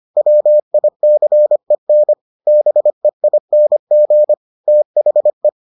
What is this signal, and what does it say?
Morse code, heard as a keyed tone, about 25 wpm, 600 Hz. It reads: WICEN BEING THE